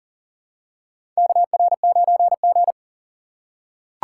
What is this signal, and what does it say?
Morse code, heard as a keyed tone, about 40 wpm, 700 Hz. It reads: KR9G
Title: USA Call Signs - 4 Characters - 4x Character Spacing - 40wpm